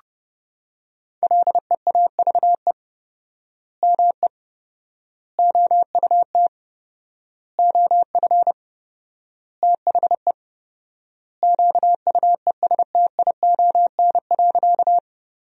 Morse code, heard as a keyed tone, about 30 wpm, 700 Hz. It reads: LEAVE ME OUT OF THE QUESTION.
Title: 4x Word-Spacing - Sentences from Top 400 Words - 30wpm